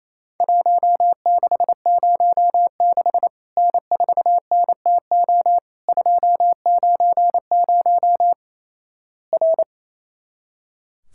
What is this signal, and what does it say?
Morse code, heard as a keyed tone, about 28 wpm, 700 Hz. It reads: 1606 N4NTO 290 R
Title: Worked All Europe DX Contest - QTC Exchange - 28wpm